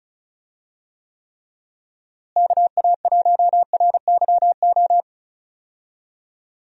Morse code, heard as a keyed tone, about 34 wpm, 700 Hz. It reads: KA1RYO
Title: USA Call Signs - 6 Characters - Repeated 3x - 35wpm